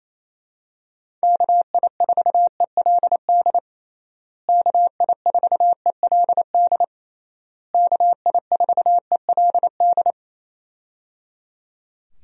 Morse code, heard as a keyed tone, about 27 wpm, 700 Hz. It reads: KI4ELD KI4ELD KI4ELD
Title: USA Call Signs - 6 Characters - Repeated 3x - 28wpm